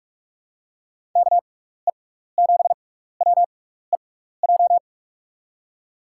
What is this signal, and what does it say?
Morse code, heard as a keyed tone, about 46 wpm, spaced as for 16 wpm, 700 Hz. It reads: KE7WEJ
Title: USA Call Signs - 6 Characters - 6x Character Spacing - 45wpm